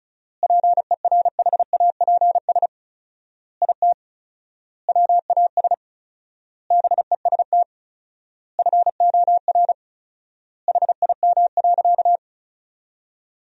Morse code, heard as a keyed tone, about 34 wpm, 700 Hz. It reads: PERHAPS IT WAS BEST FOR HIM.
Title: 4x Word-Spacing - Sentences from Top 500 Words - 35wpm